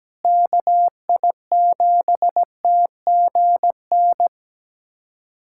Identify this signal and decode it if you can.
Morse code, heard as a keyed tone, about 17 wpm, 700 Hz. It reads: KI7TGN
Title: USA Callsign DE Callsign - 6 Characters - 17wpm